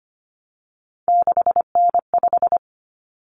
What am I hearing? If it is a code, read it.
Morse code, heard as a keyed tone, about 25 wpm, 700 Hz. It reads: 6N5
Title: Common Cut Numbers - 3-Digit 25wpm